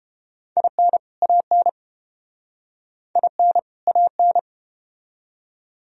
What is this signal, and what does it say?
Morse code, heard as a keyed tone, about 31 wpm, 700 Hz. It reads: IN AN IN AN